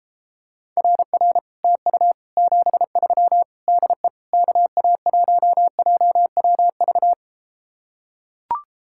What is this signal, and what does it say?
Morse code, heard as a keyed tone, about 33 wpm, 700 Hz. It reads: RR TU 73 DE KA1JWV